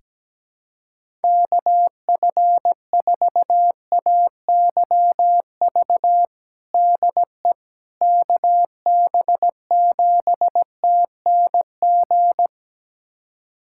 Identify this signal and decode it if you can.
Morse code, heard as a keyed tone, about 17 wpm, 700 Hz. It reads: KF4AYV DE KB7TNG
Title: USA Callsign DE Callsign - 6 Characters - 17wpm